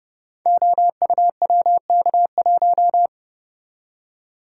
Morse code, heard as a keyed tone, about 30 wpm, 700 Hz. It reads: OUWK1
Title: Code Groups - 5 Letter:Numbers - 30wpm